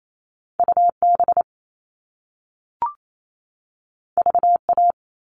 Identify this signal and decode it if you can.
Morse code, heard as a keyed tone, about 28 wpm, 700 Hz. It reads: UB VA
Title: All Cut Numbers - 2-Digit 28wpm